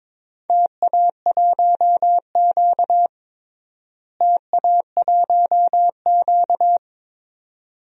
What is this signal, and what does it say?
Morse code, heard as a keyed tone, about 22 wpm, 700 Hz. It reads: TA1Q TA1Q
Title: DXCC-100 Call Signs - 4 Characters - Repeated 2x - 22wpm